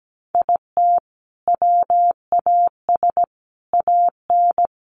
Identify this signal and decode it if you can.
Morse code, heard as a keyed tone, about 17 wpm, 700 Hz. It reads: IT WAS AN
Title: Sentences from Top 1500 Words 17wpm